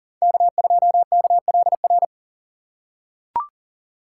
Morse code, heard as a keyed tone, about 40 wpm, 700 Hz. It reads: K2KLR
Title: USA Call Signs - 5 Characters - 2x Character Spacing - 40wpm